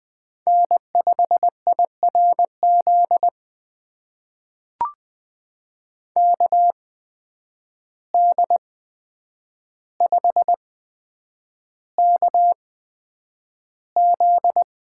Morse code, heard as a keyed tone, about 20 wpm, spaced as for 9 wpm, 700 Hz. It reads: N5IRZ KD5KZ
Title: USA Call Signs - 5 Characters - 8x Character Spacing - 20wpm